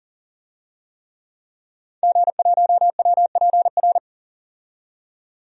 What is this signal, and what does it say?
Morse code, heard as a keyed tone, about 40 wpm, 700 Hz. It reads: G1WPR
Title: DXCC-100 Call Signs - 5 Characters - 40wpm